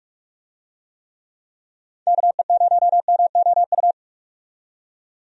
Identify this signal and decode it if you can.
Morse code, heard as a keyed tone, about 43 wpm, 700 Hz. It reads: KE0MOU
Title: USA Call Signs - 6 Characters - 45wpm